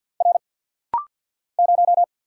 Morse code, heard as a keyed tone, about 50 wpm, 700 Hz. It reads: R 0
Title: Single Letter-Number - Rapid-Fire 50wpm